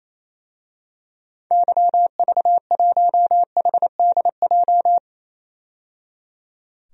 Morse code, heard as a keyed tone, about 27 wpm, 700 Hz. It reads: YV1HDJ
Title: DXCC-100 Call Signs - 6 Characters - 28wpm